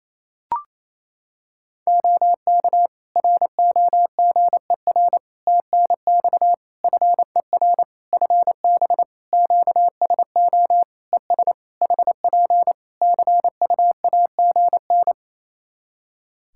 Morse code, heard as a keyed tone, about 28 wpm, 700 Hz. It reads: OK ROGER TNX FER FB QSO ES HP CUAGN
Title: Practice QSO - Exchange 4 - Ending - 28wpm